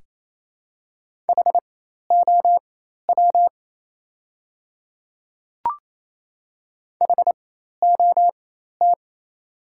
Morse code, heard as a keyed tone, about 28 wpm, spaced as for 13 wpm, 700 Hz. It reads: HOW HOT
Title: 4x Character Spacing - Top 200 Words - 28wpm